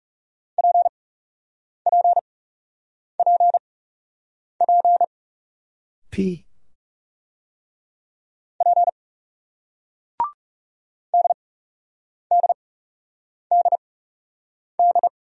Morse code, heard as a keyed tone, about 38 wpm, spaced as for 8 wpm, 700 Hz. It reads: PPPP P DDDD